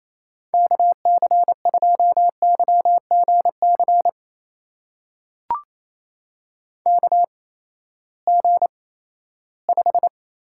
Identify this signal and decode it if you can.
Morse code, heard as a keyed tone, about 27 wpm, spaced as for 16 wpm, 700 Hz. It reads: KC2YGC KG5
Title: USA Call Signs - 6 Characters - 8x Character Spacing - 28wpm